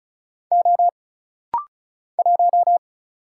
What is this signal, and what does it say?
Morse code, heard as a keyed tone, about 34 wpm, 700 Hz. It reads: O 1
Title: Single Letter-Number Alt - Rapid-Fire 35wpm